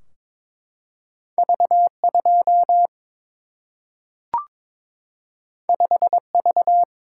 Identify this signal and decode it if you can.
Morse code, heard as a keyed tone, about 22 wpm, 700 Hz. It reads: V2 5V
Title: Single-Letter Single-Number 22wpm